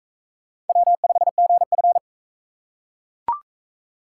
Morse code, heard as a keyed tone, about 43 wpm, 700 Hz. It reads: W5GF